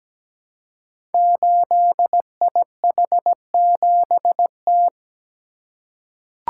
Morse code, heard as a keyed tone, about 17 wpm, 700 Hz. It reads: 8IH7T
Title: Code Groups - 5 Letter:Numbers - 17wpm